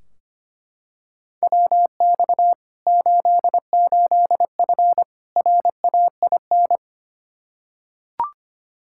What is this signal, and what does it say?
Morse code, heard as a keyed tone, about 25 wpm, 700 Hz. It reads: WX 88F RAIN
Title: Practice QSO - Exchange 2 - Info - 25wpm